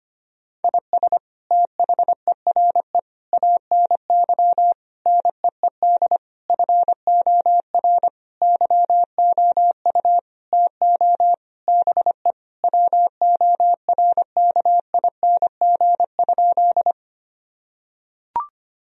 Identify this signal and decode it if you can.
Morse code, heard as a keyed tone, about 25 wpm, 700 Hz. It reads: IS THERE ANY NEED FOR YOU TO BE WORKING?
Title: Sentences from Top 1000 Words - Review Part 4 - 25wpm